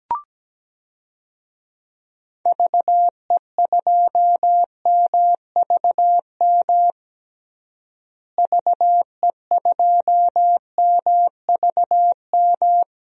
Morse code, heard as a keyed tone, about 17 wpm, 700 Hz. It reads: VE2MVM VE2MVM
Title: DXCC-100 Call Signs - 6 Characters - Repeated 2x - 17wpm